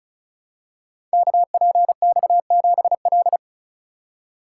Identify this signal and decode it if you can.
Morse code, heard as a keyed tone, about 36 wpm, 690 Hz. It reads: KPX7L
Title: Code Groups - 5 Letter:Numbers - 35wpm